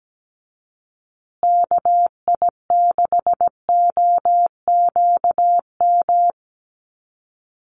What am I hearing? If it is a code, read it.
Morse code, heard as a keyed tone, about 17 wpm, 700 Hz. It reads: KI6OQM
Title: USA Call Signs Repeated 3x 17wpm